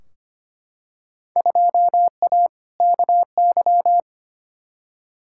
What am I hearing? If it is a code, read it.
Morse code, heard as a keyed tone, about 25 wpm, 700 Hz. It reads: 2A KY
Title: Field Day Exchanges - 25wpm